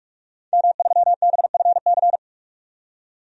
Morse code, heard as a keyed tone, about 45 wpm, 700 Hz. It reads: M3BFC
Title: DXCC-100 Call Signs - 5 Characters - 45wpm